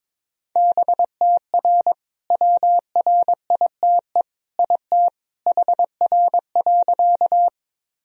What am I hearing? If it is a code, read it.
Morse code, heard as a keyed tone, about 22 wpm, 700 Hz. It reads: BTR WRITE IT HR.
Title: Sentences from Top 400 Words and Abbreviations 22wpm